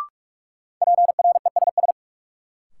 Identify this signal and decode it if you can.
Morse code, heard as a keyed tone, about 46 wpm, 700 Hz. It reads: PRESS
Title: Sentences from Top 800 Words - New Word Drill - 45wpm